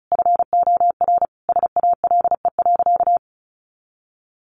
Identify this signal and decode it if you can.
Morse code, heard as a keyed tone, about 34 wpm, 700 Hz. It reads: FOR SALE.
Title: Sentences from Top 1400 Words 35wpm